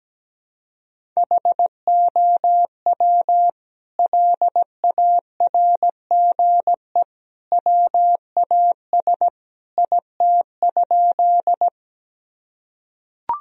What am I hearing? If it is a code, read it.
Morse code, heard as a keyed tone, about 17 wpm, 700 Hz. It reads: HOW LARGE WAS IT?
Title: Sentences from Top 1000 Words - Review Part 1 - 17wpm